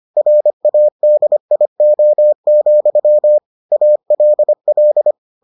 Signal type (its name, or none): Morse code